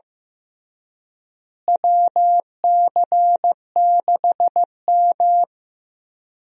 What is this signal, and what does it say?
Morse code, heard as a keyed tone, about 16 wpm, 700 Hz. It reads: WC6M
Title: USA Call Signs - 4 Characters - Repeated 2x - 15wpm